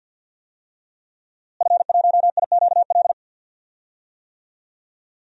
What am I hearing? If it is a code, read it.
Morse code, heard as a keyed tone, about 46 wpm, 700 Hz. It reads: F1IQL
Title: DXCC-100 Call Signs - 5 Characters - Repeated 2x - 50wpm